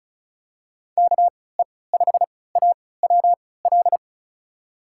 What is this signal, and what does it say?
Morse code, heard as a keyed tone, about 34 wpm, spaced as for 20 wpm, 700 Hz. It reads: KE5AWL
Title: USA Call Signs - 6 Characters - 3x Character Spacing - 35wpm